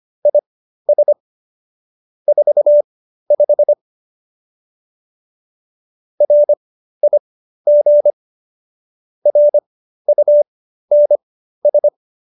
Morse code, heard as a keyed tone, about 25 wpm, spaced as for 13 wpm, 590 Hz. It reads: IS 45 RIG RUNS